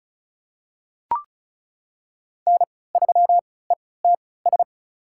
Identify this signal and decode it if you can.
Morse code, heard as a keyed tone, about 36 wpm, spaced as for 20 wpm, 700 Hz. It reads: N3ETS
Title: USA Call Signs - 5 Characters - 3x Character Spacing - 35wpm